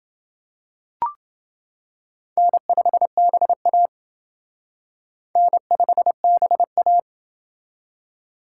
Morse code, heard as a keyed tone, about 28 wpm, 700 Hz. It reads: N5BA N5BA